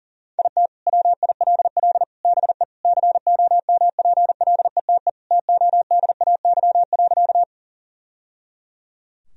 Morse code, heard as a keyed tone, about 40 wpm, 700 Hz. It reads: IT WILL BE COMPLETE TODAY.
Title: Sentences from Top 1000 Words - Review Part 2 - 40wpm